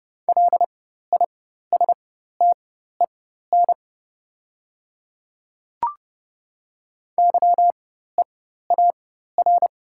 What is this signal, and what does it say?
Morse code, heard as a keyed tone, about 30 wpm, spaced as for 14 wpm, 700 Hz. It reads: LISTEN YEAR
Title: 4x Character Spacing - Top 500 Words - 30wpm